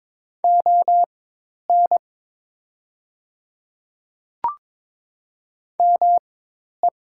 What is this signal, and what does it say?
Morse code, heard as a keyed tone, about 22 wpm, spaced as for 10 wpm, 700 Hz. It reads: ON ME
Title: 4x Character Spacing - Top 200 Words - 22wpm